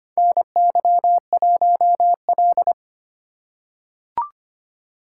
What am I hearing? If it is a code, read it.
Morse code, heard as a keyed tone, about 25 wpm, 700 Hz. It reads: NY1L
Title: USA Call Signs - 4 Characters - 3x Character Spacing - 25wpm